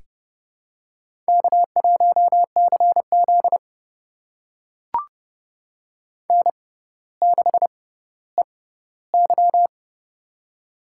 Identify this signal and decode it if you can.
Morse code, heard as a keyed tone, about 30 wpm, spaced as for 15 wpm, 700 Hz. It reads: K1CZ N6EY
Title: USA Call Signs - 4 Characters - 6x Character Spacing - 30wpm